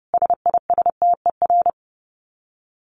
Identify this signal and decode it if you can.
Morse code, heard as a keyed tone, about 30 wpm, 700 Hz. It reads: SISTER
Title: Dit Challenge 30wpm